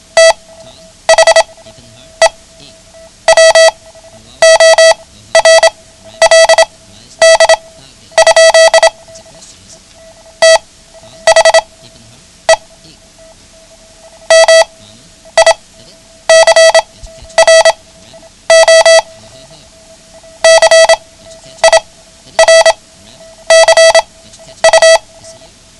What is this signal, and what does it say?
Morse code, heard as a keyed tone, about 27 wpm, spaced as for 10 wpm, 650 Hz. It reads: THE WORLD? THE MICRO CIRCU
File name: ham radio / CodeQuick / Code Quick 09